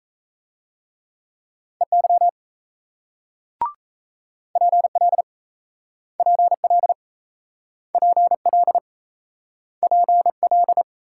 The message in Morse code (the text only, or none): EY PL PL PL PL